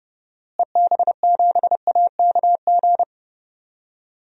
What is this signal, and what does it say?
Morse code, heard as a keyed tone, about 30 wpm, 700 Hz. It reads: EB7AKG
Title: DXCC-100 Call Signs - 6 Characters - Repeated 3x - 30wpm